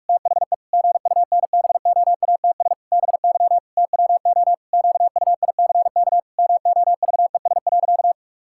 Morse code, heard as a keyed tone, about 41 wpm, 700 Hz. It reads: THE GUNBOATS BY TWO QUICK MOVES.
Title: Pangram Challenge v2 - 45wpm